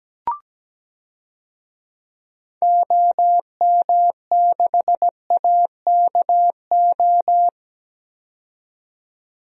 Morse code, heard as a keyed tone, about 17 wpm, 700 Hz. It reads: OM6AKO
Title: DXCC-100 Call Signs - 6 Characters - 17wpm